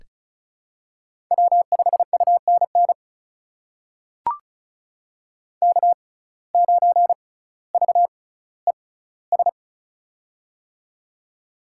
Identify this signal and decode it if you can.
Morse code, heard as a keyed tone, about 36 wpm, spaced as for 18 wpm, 700 Hz. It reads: W5UNN K9VES
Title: USA Call Signs - 5 Characters - 6x Character Spacing - 35wpm